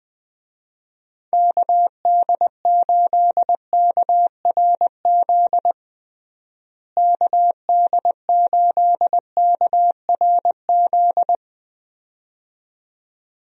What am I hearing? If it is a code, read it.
Morse code, heard as a keyed tone, about 21 wpm, 700 Hz. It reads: KD8KRZ KD8KRZ
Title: USA Call Signs - 6 Characters - Repeated 2x - 20wpm